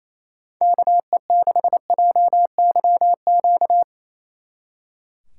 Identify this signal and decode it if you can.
Morse code, heard as a keyed tone, about 34 wpm, 700 Hz. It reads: KE6JYQ
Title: Speed Racing - USA Call Signs 28wpm